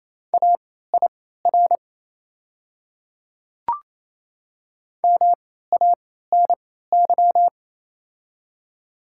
Morse code, heard as a keyed tone, about 28 wpm, spaced as for 16 wpm, 690 Hz. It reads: AIR MANY
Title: 3x Character Spacing - Top 200 Words - 28wpm